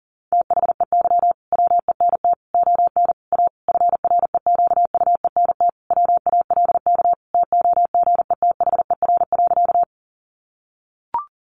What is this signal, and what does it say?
Morse code, heard as a keyed tone, about 40 wpm, 700 Hz. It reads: THEY WENT ON A FREQUENT WALK TOGETHER.
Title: Sentences from Top 1400 Words 40wpm